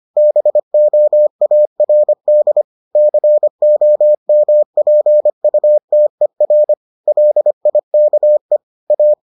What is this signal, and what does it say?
Morse code, heard as a keyed tone, about 25 wpm, 600 Hz. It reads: BOARD COMPUTER LIKE A